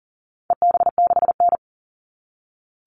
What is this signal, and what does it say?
Morse code, heard as a keyed tone, about 40 wpm, 700 Hz. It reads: EB6N
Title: All Cut Numbers - 4-Digit 40wpm